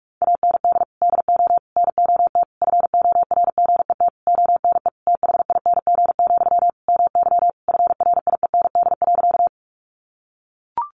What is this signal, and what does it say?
Morse code, heard as a keyed tone, about 44 wpm, 700 Hz. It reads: AND DO NOT FORGET ONE THING, MY FRIEND.
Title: Sentences from Top 1200 Words 45wpm